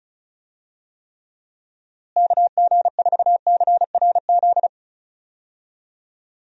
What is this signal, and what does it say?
Morse code, heard as a keyed tone, about 36 wpm, 700 Hz. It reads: KG4CRZ
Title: USA Call Signs - 6 Characters - Repeated 2x - 35wpm